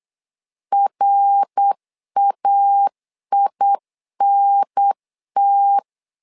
Morse code, heard as a keyed tone, about 8 wpm, 800 Hz. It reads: RAINT
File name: LEZIONI / 3 MWRO / 3 MWR 8wpm 2444